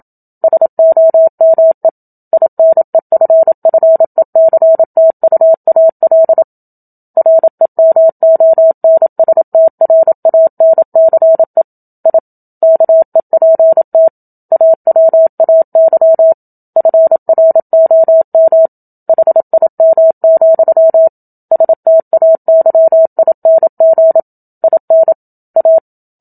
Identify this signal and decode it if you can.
Morse code, heard as a keyed tone, about 27 wpm, 650 Hz. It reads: SOME INEFFECTUAL REMONSTRANCE I KEPT AWAY FROM HIM, STAYING IN A
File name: Book Two / 01-Chapter 1 - 27 WPM